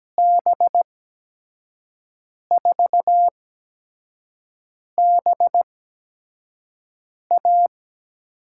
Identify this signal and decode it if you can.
Morse code, heard as a keyed tone, about 17 wpm, spaced as for 5 wpm, 700 Hz. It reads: B4BA
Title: USA Call Signs - 6 Characters - 8x Character Spacing - 17wpm